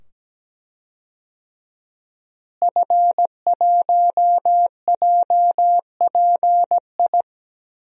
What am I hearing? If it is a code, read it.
Morse code, heard as a keyed tone, about 17 wpm, 700 Hz. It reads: F1JPI
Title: DXCC-100 Call Signs - 5 Characters - Repeated 2x - 17wpm